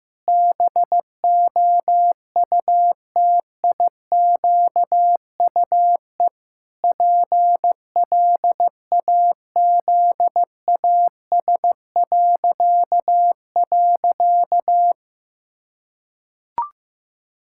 Morse code, heard as a keyed tone, about 15 wpm, 700 Hz. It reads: BOUTIQUE PLAZAS..
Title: Pangram Challenge v2 - 15wpm